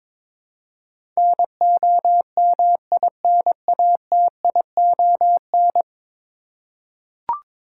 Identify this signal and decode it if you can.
Morse code, heard as a keyed tone, about 22 wpm, 700 Hz. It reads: NOMINATION
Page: morse-code-ninja-practice-22wpm